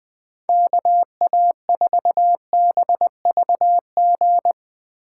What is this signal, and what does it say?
Morse code, heard as a keyed tone, about 20 wpm, 700 Hz. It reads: KA4BVG
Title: K1USN SST QSO Practice - 20wpm